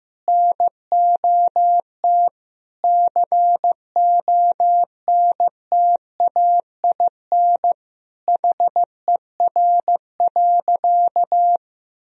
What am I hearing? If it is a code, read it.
Morse code, heard as a keyed tone, about 15 wpm, 700 Hz. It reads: NOT CONTAIN HER.
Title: Sentences from Top 1000 Words - Review Part 2 - 15wpm